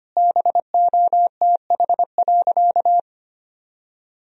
Morse code, heard as a keyed tone, about 25 wpm, 700 Hz. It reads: BOTH.
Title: Sentences from Top 1000 Words - Review Part 4 - 25wpm